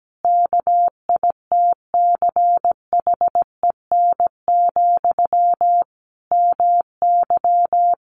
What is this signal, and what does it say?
Morse code, heard as a keyed tone, about 17 wpm, 700 Hz. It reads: KITCHEN, MY
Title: Sentences from Top 1400 Words 17wpm